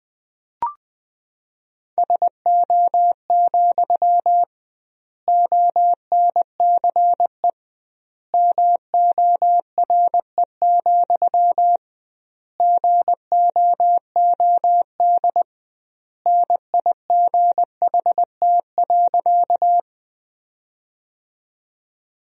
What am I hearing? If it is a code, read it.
Morse code, heard as a keyed tone, about 20 wpm, 700 Hz. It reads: SO, ONCE MORE, GOOD NIGHT.
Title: 2x Word-Spacing - Sentences from Top 300 Words - 20wpm